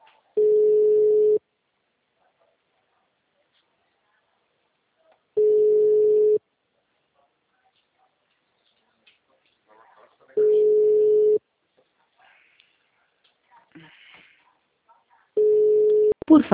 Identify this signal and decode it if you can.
Ringback tone (Europe, 425 Hz), with speech.